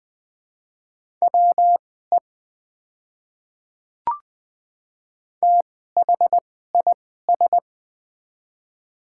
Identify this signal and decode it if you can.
Morse code, heard as a keyed tone, about 20 wpm, spaced as for 14 wpm, 700 Hz. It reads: WE THIS
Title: 2x Character Spacing - Top 100 Words - 20wpm